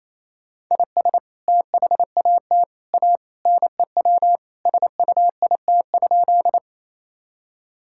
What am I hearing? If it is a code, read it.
Morse code, heard as a keyed tone, about 28 wpm, 700 Hz. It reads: IS THAT A NEW SUIT?
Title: Sentences from Top 1600 Words 28wpm